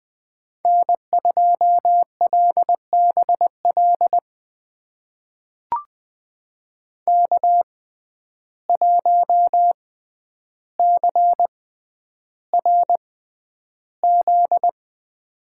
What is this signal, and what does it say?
Morse code, heard as a keyed tone, about 20 wpm, spaced as for 10 wpm, 700 Hz. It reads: N2LBL K1CRZ
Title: USA Call Signs - 5 Characters - 6x Character Spacing - 20wpm